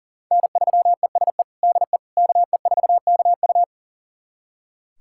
Morse code, heard as a keyed tone, about 40 wpm, 700 Hz. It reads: N3ESE DE KE4KU
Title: USA Callsign DE Callsign - 5 Characters - 40wpm